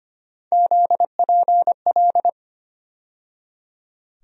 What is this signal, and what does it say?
Morse code, heard as a keyed tone, about 25 wpm, 700 Hz. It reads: ZPL
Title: USA Call Sign Postfixes - 3 Letters 25wpm